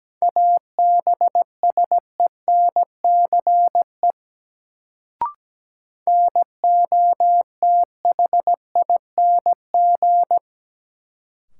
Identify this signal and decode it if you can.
Morse code, heard as a keyed tone, about 17 wpm, 700 Hz. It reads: ABSENCE NOTHING
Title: Sets of 4 Words - New Word Drill - 17wpm